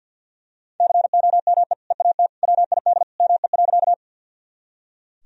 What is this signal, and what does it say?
Morse code, heard as a keyed tone, about 46 wpm, 700 Hz. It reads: COME EAT WID ME.
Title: Sentences from Top 300 Words and Abbreviations 50wpm